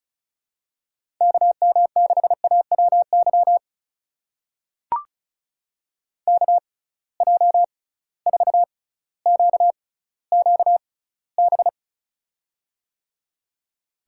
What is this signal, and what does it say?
Morse code, heard as a keyed tone, about 34 wpm, spaced as for 18 wpm, 700 Hz. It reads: KM6AWY KJ4QQB
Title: USA Call Signs - 6 Characters - 6x Character Spacing - 35wpm